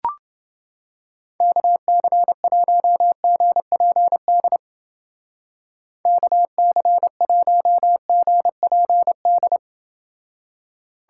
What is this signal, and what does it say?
Morse code, heard as a keyed tone, about 29 wpm, 700 Hz. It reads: KC1GPD KC1GPD